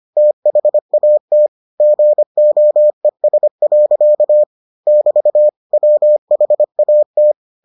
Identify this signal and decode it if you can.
Morse code, heard as a keyed tone, about 25 wpm, 600 Hz. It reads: THAT GOES. <BT> WHAT